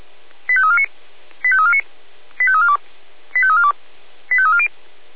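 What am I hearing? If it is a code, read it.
Five-tone selective call (ZVEI), heard as quick runs of five tones, tones of about 70 ms. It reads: 85349 85349 85342 85342 85340
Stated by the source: ZVEI five-tone selective call received on an SDR.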